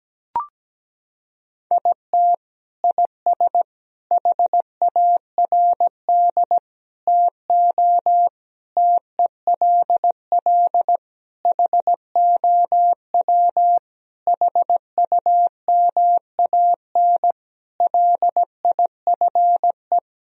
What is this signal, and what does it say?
Morse code, heard as a keyed tone, about 17 wpm, 700 Hz. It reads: IT IS HARD TO TELL HOW HUMAN LIFE
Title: Sentences from Top 1000 Words - Review Part 3 - 17wpm